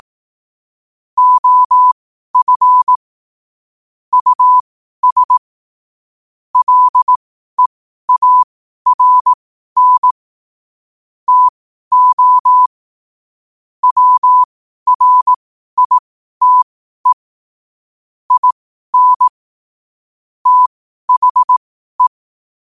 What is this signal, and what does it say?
Morse code, heard as a keyed tone, about 18 wpm, spaced as for 12 wpm, 1000 Hz. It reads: OF US LEARN TO WRITE IN THE
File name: qotdfiles / 12wpm